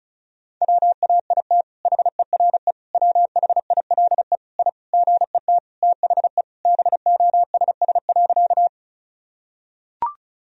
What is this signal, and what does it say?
Morse code, heard as a keyed tone, about 36 wpm, 700 Hz. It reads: WAIT HERE WHILE I GET THE BOSS.
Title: Sentences from Top 1000 Words - Review Part 4 - 35wpm